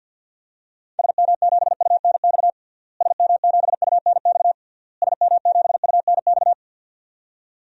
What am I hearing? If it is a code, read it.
Morse code, heard as a keyed tone, about 46 wpm, 700 Hz. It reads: SM7UNX SM7UNX SM7UNX
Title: DXCC-100 Call Signs - 6 Characters - Repeated 3x - 50wpm